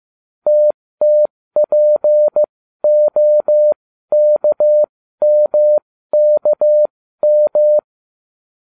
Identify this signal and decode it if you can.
Morse code, heard as a keyed tone, about 15 wpm, spaced as for 13 wpm, 600 Hz. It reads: TTPOKMKM